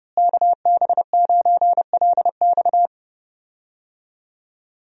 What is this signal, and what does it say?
Morse code, heard as a keyed tone, about 32 wpm, 700 Hz. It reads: KB9LX
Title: USA Call Signs - 5 Characters - Repeated 2x - 30wpm